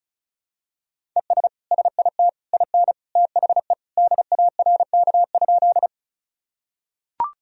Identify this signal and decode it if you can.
Morse code, heard as a keyed tone, about 36 wpm, 700 Hz. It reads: ES SIT IN THE DARK?
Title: Sentences from Top 500 Words and Abbreviations 35wpm